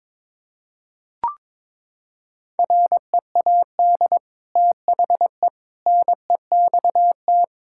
Morse code, heard as a keyed tone, about 22 wpm, 700 Hz. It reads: READ THE NEXT
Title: Sentences from Top 1000 Words - Review Part 5 - 22wpm